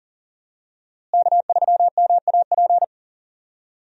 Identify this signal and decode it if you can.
Morse code, heard as a keyed tone, about 40 wpm, 700 Hz. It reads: K3MAP
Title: USA Call Signs - 5 Characters - Repeated 2x - 40wpm